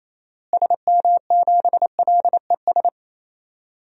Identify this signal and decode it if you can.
Morse code, heard as a keyed tone, about 28 wpm, 700 Hz. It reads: SM7LES
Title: DXCC-100 Call Signs - 6 Characters - Repeated 2x - 28wpm